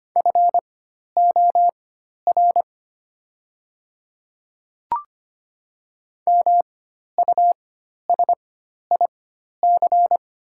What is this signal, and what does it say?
Morse code, heard as a keyed tone, about 25 wpm, spaced as for 12 wpm, 700 Hz. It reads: FOR MUSIC
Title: 4x Character Spacing - Top 300 Words - 25wpm